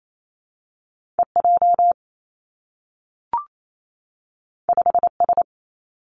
Morse code, heard as a keyed tone, about 28 wpm, 700 Hz. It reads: E J 5 S
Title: Two Challenging Letter-Number Combinations 28wpm